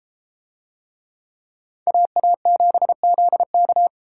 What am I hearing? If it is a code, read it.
Morse code, heard as a keyed tone, about 31 wpm, 700 Hz. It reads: AA7ZK